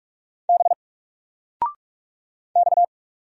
Morse code, heard as a keyed tone, about 44 wpm, 700 Hz. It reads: B X
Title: Single Character (Letter:Number:Punctuation:Prosign) 45wpm